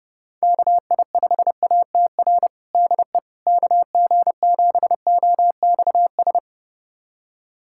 Morse code, heard as a keyed tone, about 32 wpm, 700 Hz. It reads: KI5ATR DE KG7OXS
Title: USA Callsign DE Callsign 30wpm